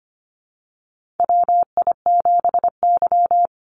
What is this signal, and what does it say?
Morse code, heard as a keyed tone, about 25 wpm, 700 Hz. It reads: WI7Y
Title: USA Call Signs 25wpm